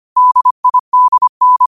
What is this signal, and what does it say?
Morse code, heard as a keyed tone, about 25 wpm, 1000 Hz. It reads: DIDN
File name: qotdfiles / 25wpm